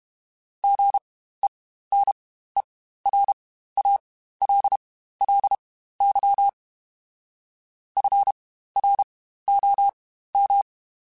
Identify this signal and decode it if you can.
Morse code, heard as a keyed tone, about 32 wpm, spaced as for 15 wpm, 800 Hz. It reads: GENERALLY FROM
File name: Spaced Characters / Spaced Characters-mp3s / Weather 5 5 sp @30